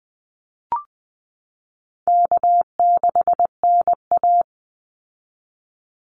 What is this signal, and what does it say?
Morse code, heard as a keyed tone, about 19 wpm, 700 Hz. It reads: K6NA
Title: Easy Call Signs Repeated 2x - 20wpm